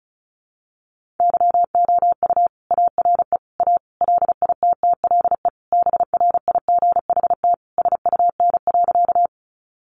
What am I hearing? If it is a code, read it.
Morse code, heard as a keyed tone, about 34 wpm, 700 Hz. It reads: YOU ARE A LITTLE BRIGHT SUN.
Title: Sentences from Top 1000 Words 35wpm